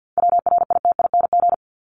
Morse code, heard as a keyed tone, about 48 wpm, 700 Hz. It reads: WRITING
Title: ING Suffix - Top 100 Words with Base Words 50wpm